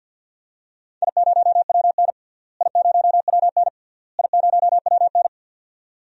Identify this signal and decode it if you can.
Morse code, heard as a keyed tone, about 50 wpm, 700 Hz. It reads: I0WN I0WN I0WN
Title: DXCC-100 Call Signs - 4 Characters - Repeated 3x - 50wpm